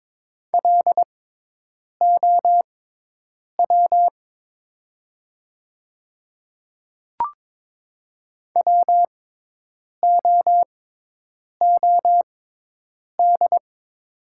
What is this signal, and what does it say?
Morse code, heard as a keyed tone, about 22 wpm, spaced as for 8 wpm, 700 Hz. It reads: LOW WOOD
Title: 6x Character Spacing - Top 300 Words - 22wpm